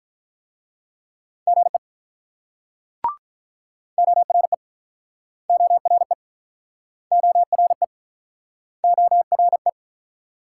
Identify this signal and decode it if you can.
Morse code, heard as a keyed tone, about 48 wpm, 700 Hz. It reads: GE ORE ORE ORE ORE